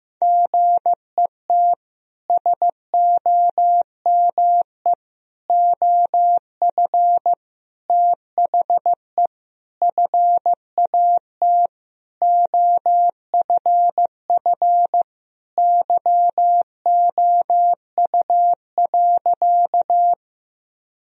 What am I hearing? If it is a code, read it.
Morse code, heard as a keyed tone, about 15 wpm, 700 Hz. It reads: GET SOME OF THE FAT OFF YOU.
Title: Sentences from Top 1000 Words - Review Part 4 - 15wpm